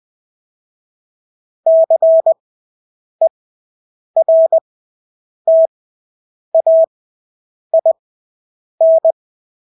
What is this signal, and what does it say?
Morse code, heard as a keyed tone, about 20 wpm, spaced as for 8 wpm, 650 Hz. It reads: CERTAIN
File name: Engelske ord 20/8 wpm